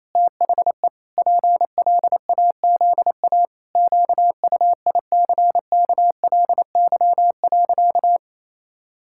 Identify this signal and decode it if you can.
Morse code, heard as a keyed tone, about 28 wpm, 700 Hz. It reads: THE PLAZA QUICKLY.
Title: Pangram Challenge v2 - 28wpm